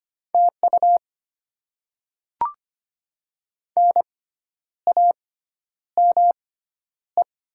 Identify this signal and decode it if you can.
Morse code, heard as a keyed tone, about 25 wpm, spaced as for 10 wpm, 700 Hz. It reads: TU NAME
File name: Top 20 QSO Elements - 6x Character Spacing - 25wpm